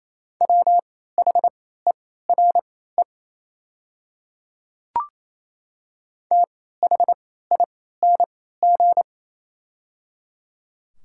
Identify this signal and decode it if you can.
Morse code, heard as a keyed tone, about 28 wpm, spaced as for 16 wpm, 700 Hz. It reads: WHERE THING
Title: 3x Character Spacing - Top 200 Words - 28wpm